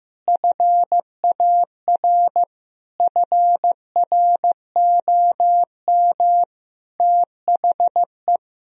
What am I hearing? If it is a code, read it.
Morse code, heard as a keyed tone, about 15 wpm, 700 Hz. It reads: FAR FROM THE
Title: Sentences from Top 1000 Words - Review Part 1 - 15wpm